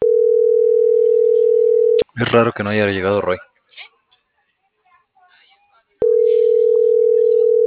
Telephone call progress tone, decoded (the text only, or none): ringback tone